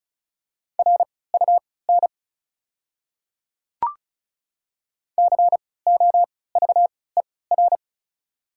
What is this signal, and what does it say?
Morse code, heard as a keyed tone, about 36 wpm, spaced as for 20 wpm, 700 Hz. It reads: RUN COVER
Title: 3x Character Spacing - Top 300 Words - 35wpm